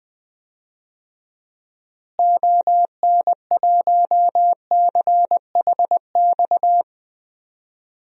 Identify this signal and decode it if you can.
Morse code, heard as a keyed tone, about 20 wpm, 700 Hz. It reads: ON1CHX
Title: DXCC-100 Call Signs - 6 Characters - Repeated 2x - 20wpm